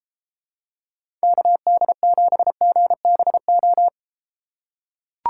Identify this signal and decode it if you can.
Morse code, heard as a keyed tone, about 32 wpm, 700 Hz. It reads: KD7GBO